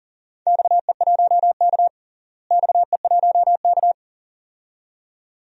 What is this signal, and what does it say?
Morse code, heard as a keyed tone, about 40 wpm, 700 Hz. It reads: XE1K XE1K
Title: DXCC-100 Call Signs - 4 Characters - Repeated 2x - 40wpm